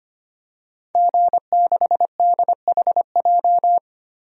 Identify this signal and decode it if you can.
Morse code, heard as a keyed tone, about 25 wpm, 700 Hz. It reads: G6DHJ
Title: DXCC-100 Call Signs - 5 Characters - Repeated 3x - 25wpm